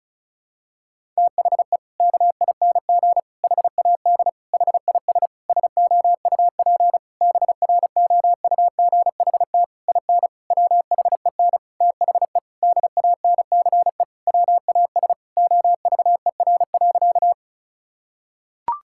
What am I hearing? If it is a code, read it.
Morse code, heard as a keyed tone, about 36 wpm, 700 Hz. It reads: THE KING HAD HIS SOUP BROUGHT IN WHEN THE DANCE WAS OVER.
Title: Sentences from Top 1000 Words - Review Part 6 - 35wpm